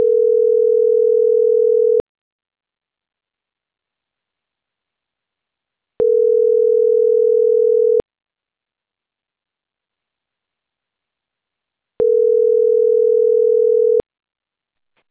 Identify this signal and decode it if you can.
Ringback tone (North America), 440 + 480 Hz.
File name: CENTRAL CAEA / out agcaea10 - .84384